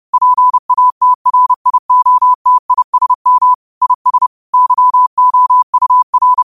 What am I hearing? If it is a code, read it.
Morse code, heard as a keyed tone, about 30 wpm, 1000 Hz. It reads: PATRIOTISM IS YOUR